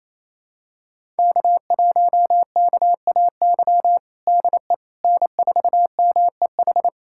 Morse code, heard as a keyed tone, about 28 wpm, 700 Hz. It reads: K1KAY DE N4MEH
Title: USA Callsign DE Callsign - 5 Characters - 28wpm